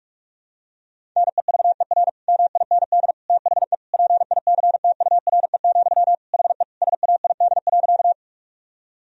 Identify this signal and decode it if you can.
Morse code, heard as a keyed tone, about 46 wpm, 700 Hz. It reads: NEVER MIND THE PICTURE, HE SAID.
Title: Sentences from Top 1000 Words - Review Part 1 - 45wpm